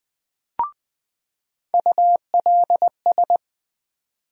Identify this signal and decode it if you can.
Morse code, heard as a keyed tone, about 20 wpm, 700 Hz. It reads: ULS
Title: USA Call Sign Postfixes - 3 Letters 20wpm